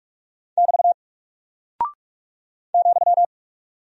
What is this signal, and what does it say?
Morse code, heard as a keyed tone, about 44 wpm, 700 Hz. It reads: <BT> ,
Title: Single Character (Letter:Number:Punctuation:Prosign) Alt 45wpm